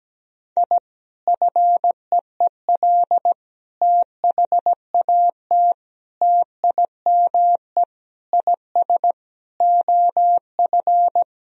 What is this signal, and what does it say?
Morse code, heard as a keyed tone, about 17 wpm, 700 Hz. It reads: I FEEL THAT TIME IS OF
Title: Sentences from Top 1000 Words - Review Part 3 - 17wpm